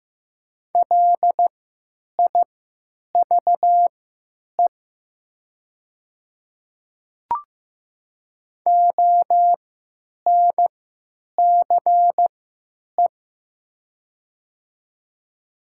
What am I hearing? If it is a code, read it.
Morse code, heard as a keyed tone, about 15 wpm, spaced as for 8 wpm, 700 Hz. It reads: LIVE ONCE
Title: 3x Character Spacing - Top 300 Words - 15wpm